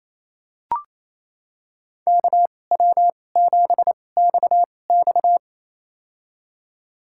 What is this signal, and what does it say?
Morse code, heard as a keyed tone, about 28 wpm, spaced as for 20 wpm, 700 Hz. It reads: KW7XX
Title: USA Call Signs - 5 Characters - 2x Character Spacing - 28wpm